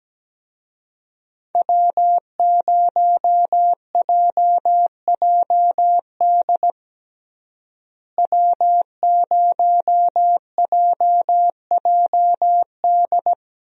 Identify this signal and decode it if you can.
Morse code, heard as a keyed tone, about 17 wpm, 700 Hz. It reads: W0JJD W0JJD
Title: USA Call Signs - 5 Characters - Repeated 3x - 17wpm